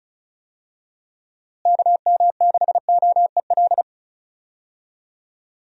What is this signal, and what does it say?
Morse code, heard as a keyed tone, about 36 wpm, 700 Hz. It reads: KM6OEL
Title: USA Call Signs - 6 Characters - Repeated 3x - 35wpm